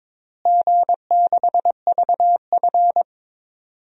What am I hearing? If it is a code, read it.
Morse code, heard as a keyed tone, about 22 wpm, 700 Hz. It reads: G6VF
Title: DXCC-100 Call Signs - 4 Characters - Repeated 3x - 22wpm